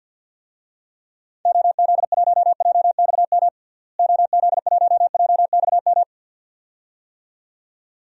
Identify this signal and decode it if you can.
Morse code, heard as a keyed tone, about 47 wpm, 700 Hz. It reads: OZ1JXM OZ1JXM
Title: DXCC-100 Call Signs - 6 Characters - Repeated 2x - 50wpm